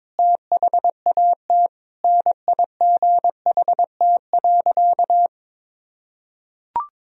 Morse code, heard as a keyed tone, about 22 wpm, 700 Hz. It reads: THAT NIGHT.
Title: Sentences from Top 1000 Words - Review Part 3 - 22wpm